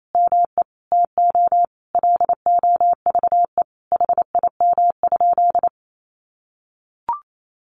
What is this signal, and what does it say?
Morse code, heard as a keyed tone, about 28 wpm, 700 Hz. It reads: ME TO LOVE HIM?
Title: Sentences from Top 1100 Words 28wpm